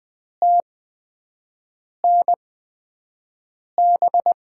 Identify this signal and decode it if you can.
Morse code, heard as a keyed tone, about 20 wpm, spaced as for 5 wpm, 700 Hz. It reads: TNB
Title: USA Call Signs - 6 Characters - 8x Character Spacing - 20wpm